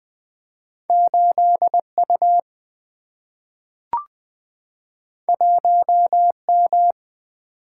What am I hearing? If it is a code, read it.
Morse code, heard as a keyed tone, about 20 wpm, 700 Hz. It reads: 8U 1M
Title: Single-Letter Single-Number 20wpm